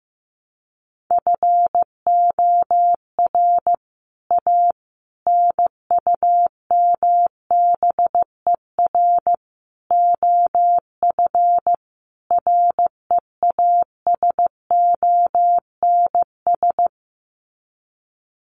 Morse code, heard as a keyed tone, about 15 wpm, 700 Hz. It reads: FOR A NUMBER OF REASONS
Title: Sets of 5 Words Encore2 15wpm